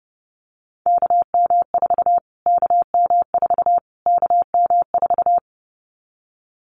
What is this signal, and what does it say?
Morse code, heard as a keyed tone, about 30 wpm, 700 Hz. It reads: KM4 KM4 KM4
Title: Callsign Prefixes for Top 25 DXCC - Repeated 3x - 30wpm